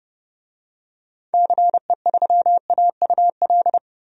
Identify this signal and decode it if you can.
Morse code, heard as a keyed tone, about 30 wpm, 700 Hz. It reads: CE3AUL
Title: DXCC-100 Call Signs - 6 Characters - Repeated 2x - 30wpm